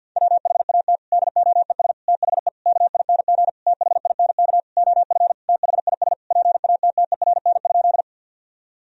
Morse code, heard as a keyed tone, about 51 wpm, 700 Hz. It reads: WHAT DOES THE KING THINK OF THIS PATTERN?
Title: Sentences from Top 1000 Words - Review Part 2 - 50wpm